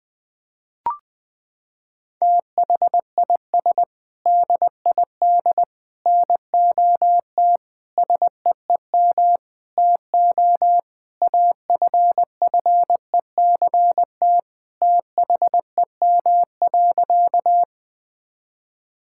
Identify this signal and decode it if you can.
Morse code, heard as a keyed tone, about 20 wpm, 700 Hz. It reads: THIS DID NOT SEEM TO AFFECT THEM.
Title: Sentences from Top 1000 Words - Review Part 6 - 20wpm